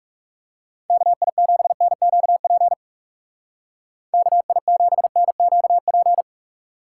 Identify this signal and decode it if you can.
Morse code, heard as a keyed tone, about 37 wpm, 700 Hz. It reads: KI7NQP KI7NQP
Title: Speed Racing - USA 2x3 Call Signs 30wpm